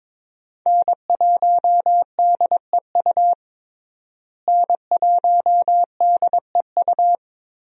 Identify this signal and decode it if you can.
Morse code, heard as a keyed tone, about 22 wpm, 700 Hz. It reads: N1DEU N1DEU
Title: USA Call Signs - 5 Characters - Repeated 2x - 22wpm